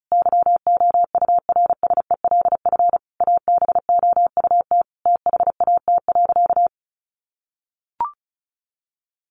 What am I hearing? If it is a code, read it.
Morse code, heard as a keyed tone, about 34 wpm, 700 Hz. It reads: YOURSELF ABOUT THAT.
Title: Sentences from Top 1200 Words 35wpm